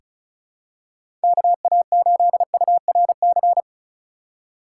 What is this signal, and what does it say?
Morse code, heard as a keyed tone, about 34 wpm, 700 Hz. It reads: KA8URC
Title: USA Call Signs - 6 Characters - 2x Character Spacing - 35wpm